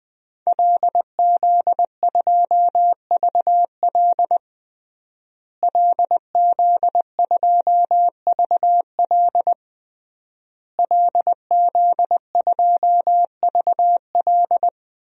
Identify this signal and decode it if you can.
Morse code, heard as a keyed tone, about 20 wpm, 700 Hz. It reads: LZ2VL LZ2VL LZ2VL
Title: DXCC-100 Call Signs - 5 Characters - Repeated 3x - 20wpm